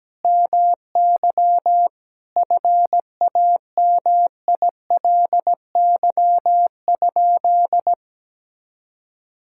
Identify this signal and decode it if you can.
Morse code, heard as a keyed tone, about 17 wpm, 700 Hz. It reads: MY FAMILY?
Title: Sentences from Top 1000 Words - Review Part 6 - 17wpm